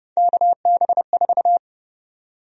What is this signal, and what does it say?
Morse code, heard as a keyed tone, about 30 wpm, 700 Hz. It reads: KB4
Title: USA Call Sign Prefixes - 30wpm